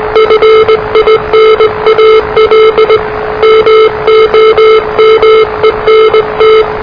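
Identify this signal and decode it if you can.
Morse code, heard as a keyed tone, about 19 wpm, 440 Hz, with noise. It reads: FINAL MOMENT
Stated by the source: Morse challenge from NZ Net, the national amateur radio net of Aotearoa/New Zealand, which meets on 3535.0 kHz.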